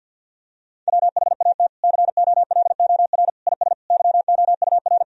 Morse code, heard as a keyed tone, about 52 wpm, 700 Hz. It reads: WHAT COLOR IS YOUR
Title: Sentences from Top 1000 Words - Review Part 3 - 50wpm